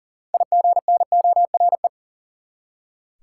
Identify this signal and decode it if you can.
Morse code, heard as a keyed tone, about 37 wpm, 700 Hz. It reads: IGNORE
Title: Six Letter Words - 40wpm